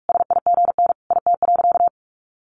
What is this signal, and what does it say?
Morse code, heard as a keyed tone, about 44 wpm, 700 Hz. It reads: SIGN IT.
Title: Sentences from Top 1000 Words 45wpm